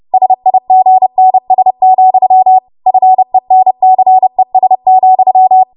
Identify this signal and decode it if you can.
Morse code, heard as a keyed tone, about 30 wpm, 750 Hz. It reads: SIGNS, FENCES,
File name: Morse / 30WPM